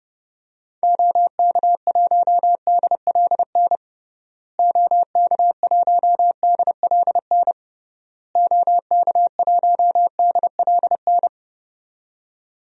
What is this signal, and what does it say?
Morse code, heard as a keyed tone, about 30 wpm, 700 Hz. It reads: OK1DLN OK1DLN OK1DLN
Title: DXCC-100 Call Signs - 6 Characters - Repeated 3x - 30wpm